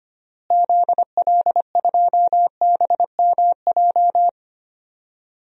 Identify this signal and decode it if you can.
Morse code, heard as a keyed tone, about 25 wpm, 700 Hz. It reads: ZL2BMJ
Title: DXCC-100 Call Signs - 6 Characters - Repeated 3x - 25wpm